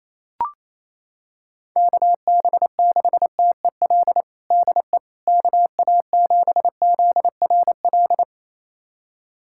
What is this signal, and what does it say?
Morse code, heard as a keyed tone, about 28 wpm, 700 Hz. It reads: KB6TEL DE KA7ZRL
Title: USA Callsign DE Callsign 28wpm